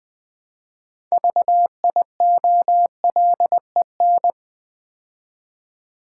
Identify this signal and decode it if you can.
Morse code, heard as a keyed tone, about 21 wpm, 690 Hz. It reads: VIOLEN
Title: Build A Word - 2 Letters At A Time - 8-Letter Words - 20wpm